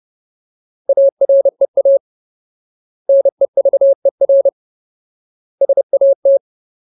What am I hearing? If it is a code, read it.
Morse code, heard as a keyed tone, about 30 wpm, 550 Hz. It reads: AREA NEVER SAT
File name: Ultimate Word List / 30WPM Ultimate